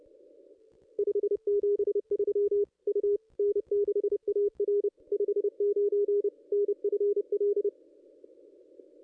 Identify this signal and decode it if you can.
Morse code, heard as a keyed tone, about 30 wpm, 410 Hz. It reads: 573 U N6AR 59 NFL
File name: contests / ss-cw-k5tr-op-n5rz -1800